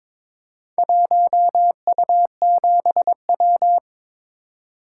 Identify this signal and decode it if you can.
Morse code, heard as a keyed tone, about 22 wpm, 690 Hz. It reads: 1U7W
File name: practice / 22wpm / Four Letter-Number Combinations - 22wpm